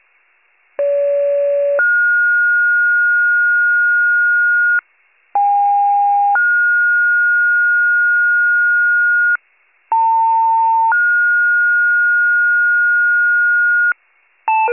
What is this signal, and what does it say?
Two-tone page, heard as a one-second tone, then a three-second tone, about 570 then 1430 Hz, about 810 then 1430 Hz, about 900 then 1430 Hz. It is heard from 0.8 to 13.9 s.